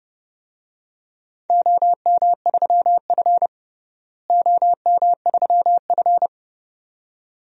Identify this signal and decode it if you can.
Morse code, heard as a keyed tone, about 32 wpm, 700 Hz. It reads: OM3F OM3F
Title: DXCC-100 Call Signs - 4 Characters - Repeated 2x - 30wpm